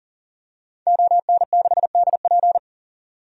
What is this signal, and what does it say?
Morse code, heard as a keyed tone, about 40 wpm, 700 Hz. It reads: ON6DP
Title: DXCC-100 Call Signs - 5 Characters - 40wpm